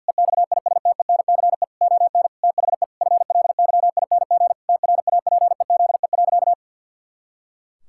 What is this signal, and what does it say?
Morse code, heard as a keyed tone, about 50 wpm, 700 Hz. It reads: EXISTENCE ON THE FLYING TRAPEZE.
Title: Pangram Challenge v2 - 50wpm